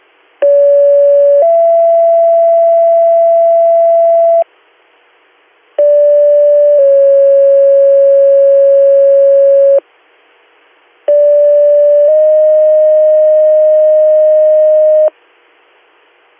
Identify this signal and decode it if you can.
Two-tone page, heard as a one-second tone, then a three-second tone, about 570 then 670 Hz, about 580 then 550 Hz, about 580 then 620 Hz. It is heard from 0.4 to 15.1 s.